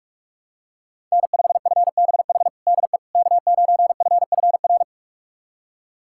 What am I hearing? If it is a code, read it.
Morse code, heard as a keyed tone, about 46 wpm, 700 Hz. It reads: N5FBH DE K9FFR
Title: USA Callsign DE Callsign - 5 Characters - 45wpm